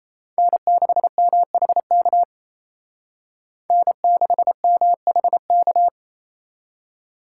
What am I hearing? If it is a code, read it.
Morse code, heard as a keyed tone, about 36 wpm, 700 Hz. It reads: N6MHK N6MHK